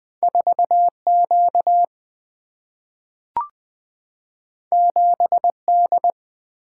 Morse code, heard as a keyed tone, about 20 wpm, 700 Hz. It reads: 4Q 7D
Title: Single-Letter Single-Number 20wpm